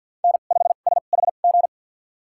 Morse code, heard as a keyed tone, about 52 wpm, spaced as for 36 wpm, 700 Hz. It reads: N5SHG
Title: USA Call Signs - 5 Characters - 2x Character Spacing - 50wpm